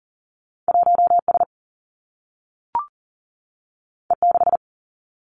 Morse code, heard as a keyed tone, about 39 wpm, 700 Hz. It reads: 1 S E 6
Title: Two Challenging Letter-Number Combinations 40wpm